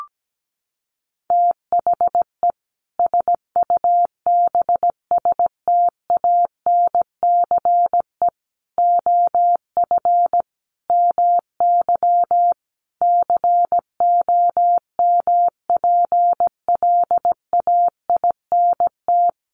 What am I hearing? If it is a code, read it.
Morse code, heard as a keyed tone, about 17 wpm, 700 Hz. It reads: THE SUBSTANCE OF MY COMPLAINT
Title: Sentences from Top 1500 Words 17wpm